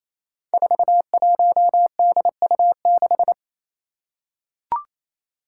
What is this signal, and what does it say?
Morse code, heard as a keyed tone, about 29 wpm, 700 Hz. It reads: 41DU6
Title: Five Challenging Letter-Number Combinations 28wpm